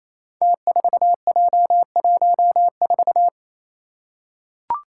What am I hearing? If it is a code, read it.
Morse code, heard as a keyed tone, about 27 wpm, 700 Hz. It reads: T4J14
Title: Five Challenging Letter-Number Combinations 28wpm